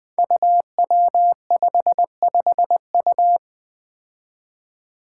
Morse code, heard as a keyed tone, about 20 wpm, 700 Hz. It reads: UW55U
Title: Five Challenging Letter-Number Combinations 20wpm